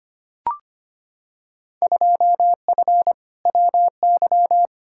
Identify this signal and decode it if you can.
Morse code, heard as a keyed tone, about 25 wpm, 700 Hz. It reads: 2F WY
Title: Field Day Exchanges Encore - 25wpm